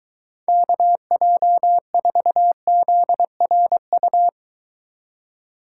Morse code, heard as a keyed tone, about 22 wpm, 700 Hz. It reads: KJ4ZRU